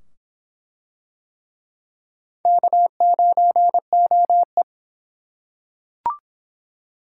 Morse code, heard as a keyed tone, about 25 wpm, 700 Hz. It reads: K9OE